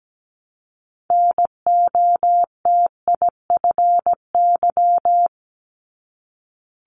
Morse code, heard as a keyed tone, about 17 wpm, 700 Hz. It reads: NOTIFY
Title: Multi-letter Phonemes 17wpm